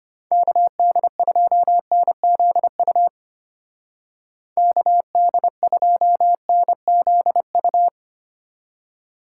Morse code, heard as a keyed tone, about 31 wpm, 700 Hz. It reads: KD2NZU KD2NZU
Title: Speed Racing - USA 2x3 Call Signs 25wpm